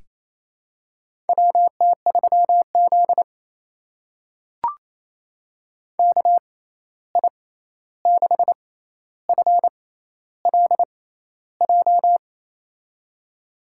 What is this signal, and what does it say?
Morse code, heard as a keyed tone, about 28 wpm, spaced as for 13 wpm, 700 Hz. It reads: WT3Z KI6FLJ
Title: USA Call Signs - 6x Character Spacing - 28wpm